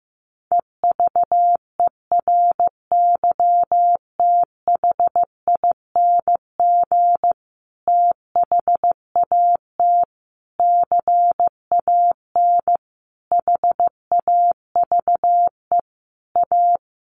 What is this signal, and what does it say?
Morse code, heard as a keyed tone, about 15 wpm, 700 Hz. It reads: EVERYTHING THAT CAN HAVE A
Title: Sentences from Top 1300 Words 15wpm